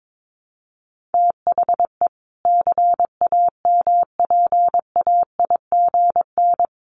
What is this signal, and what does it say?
Morse code, heard as a keyed tone, about 22 wpm, 700 Hz. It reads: THE CAMPAIGN
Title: Sentences from Top 1400 Words 22wpm